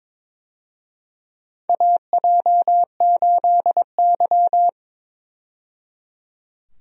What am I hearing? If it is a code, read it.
Morse code, heard as a keyed tone, about 22 wpm, 700 Hz. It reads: AJ8Y
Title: USA Call Signs - 4 Characters - 22wpm